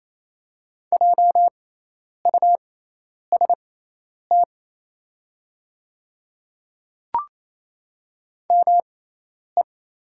Morse code, heard as a keyed tone, about 28 wpm, spaced as for 10 wpm, 700 Hz. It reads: JUST ME
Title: 6x Character Spacing - Top 200 Words - 28wpm